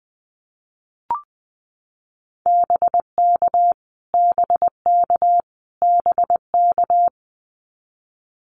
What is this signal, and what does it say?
Morse code, heard as a keyed tone, about 20 wpm, 700 Hz. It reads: BK BK BK
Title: Abbreviations Repeated 3x - 20wpm